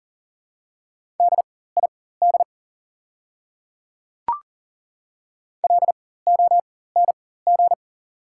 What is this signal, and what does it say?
Morse code, heard as a keyed tone, about 38 wpm, spaced as for 18 wpm, 700 Hz. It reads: DID LONG
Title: 4x Character Spacing - Top 100 Words - 40wpm